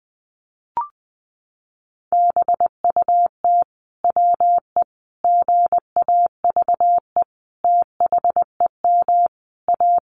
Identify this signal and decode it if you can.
Morse code, heard as a keyed tone, about 19 wpm, 700 Hz. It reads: BUT WE GAVE THEM A
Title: Sentences from Top 1200 Words 20wpm